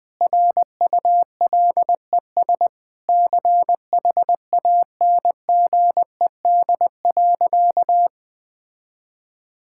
Morse code, heard as a keyed tone, about 21 wpm, 700 Hz. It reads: RULES CHANGED.
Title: Sentences with 7-Word Phrases - 20wpm